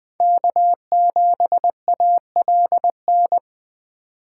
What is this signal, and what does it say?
Morse code, heard as a keyed tone, about 19 wpm, 700 Hz. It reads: K7ALN